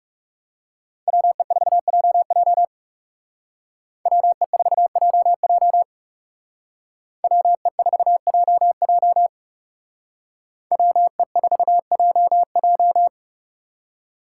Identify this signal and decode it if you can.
Morse code, heard as a keyed tone, about 36 wpm, 700 Hz. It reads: WE4JJ WE4JJ WE4JJ WE4JJ
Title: Speed Racing - USA 2x2 Call Signs 30wpm